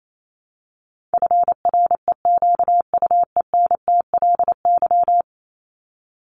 Morse code, heard as a keyed tone, about 28 wpm, 700 Hz. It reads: FREQUENTLY
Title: LY Suffix - Top 100 Words with Base Words 28wpm